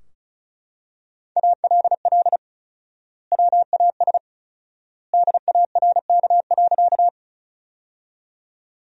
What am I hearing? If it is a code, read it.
Morse code, heard as a keyed tone, about 34 wpm, 700 Hz. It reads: ALL WAS DARK.
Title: 4x Word-Spacing - Sentences from Top 500 Words - 35wpm